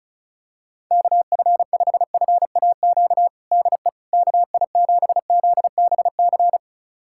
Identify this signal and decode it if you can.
Morse code, heard as a keyed tone, about 36 wpm, 700 Hz. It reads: KF5FAQ DE KI7ZBC
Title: USA Callsign DE Callsign 35wpm